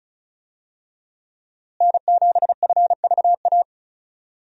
Morse code, heard as a keyed tone, about 36 wpm, 700 Hz. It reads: N7FVA
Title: USA Call Signs - 5 Characters - Repeated 2x - 35wpm